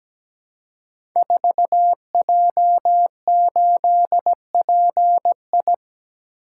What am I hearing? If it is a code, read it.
Morse code, heard as a keyed tone, about 17 wpm, 700 Hz. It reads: 4J8PI
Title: Code Groups - 5 Letter:Numbers - 17wpm